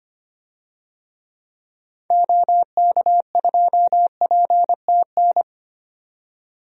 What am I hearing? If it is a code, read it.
Morse code, heard as a keyed tone, about 25 wpm, 700 Hz. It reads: OK2PTN
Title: DXCC-100 Call Signs - 6 Characters - 25wpm